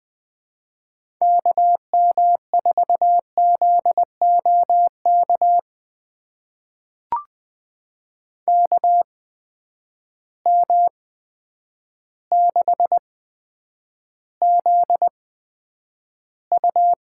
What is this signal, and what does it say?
Morse code, heard as a keyed tone, about 21 wpm, spaced as for 9 wpm, 700 Hz. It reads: KM4ZOK KM6ZU
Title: USA Call Signs - 6 Characters - 8x Character Spacing - 20wpm